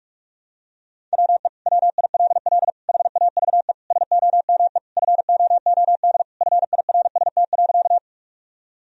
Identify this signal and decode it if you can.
Morse code, heard as a keyed tone, about 46 wpm, 700 Hz. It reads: WE WILL HAVE SOME FOOD FIRST.
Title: Sentences from Top 1000 Words - Review Part 1 - 45wpm